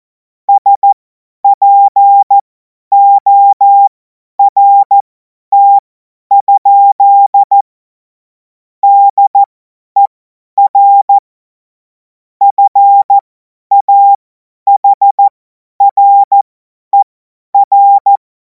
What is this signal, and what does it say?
Morse code, heard as a keyed tone, about 14 wpm, spaced as for 10 wpm, 800 Hz. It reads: SPORT? DER FAHRER
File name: Morsekurs / Witze / Witz 07. Juli 2026 50 F70